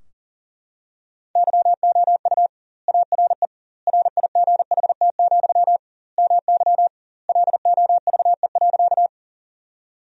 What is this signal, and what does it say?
Morse code, heard as a keyed tone, about 40 wpm, 700 Hz. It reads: YOU ARE RIGHT, MY LOVE.
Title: 2x Word-Spacing - Sentences from Top 400 Words - 40wpm